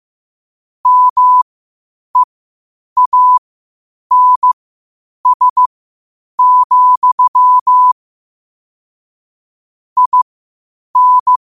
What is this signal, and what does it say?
Morse code, heard as a keyed tone, about 15 wpm, spaced as for 8 wpm, 1000 Hz. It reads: MEANS, IN
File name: qotdfiles / 8wpm